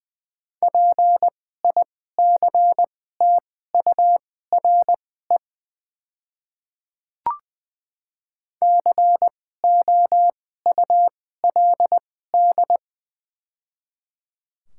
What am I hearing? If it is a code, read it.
Morse code, heard as a keyed tone, about 20 wpm, spaced as for 14 wpm, 700 Hz. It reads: PICTURE COULD
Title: 2x Character Spacing - Top 200 Words - 20wpm